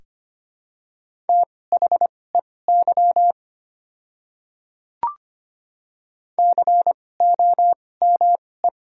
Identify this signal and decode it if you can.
Morse code, heard as a keyed tone, about 25 wpm, spaced as for 18 wpm, 690 Hz. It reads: THEY COME
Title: 2x Character Spacing - Top 100 Words - 25wpm